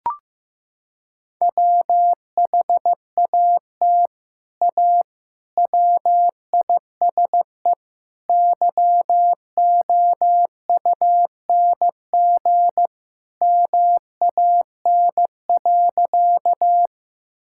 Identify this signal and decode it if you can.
Morse code, heard as a keyed tone, about 16 wpm, 700 Hz. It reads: WHAT A WISE YOUNG MAN.
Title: Sentences from Top 1600 Words 15wpm